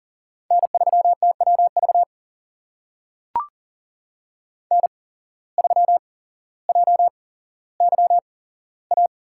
Morse code, heard as a keyed tone, about 40 wpm, spaced as for 17 wpm, 700 Hz. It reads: N3TWV N3JYA
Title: USA Call Signs - 5 Characters - 8x Character Spacing - 40wpm